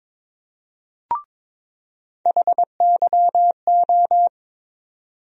Morse code, heard as a keyed tone, about 22 wpm, 700 Hz. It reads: HYO
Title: USA Call Sign Postfixes - 3 Letters 22wpm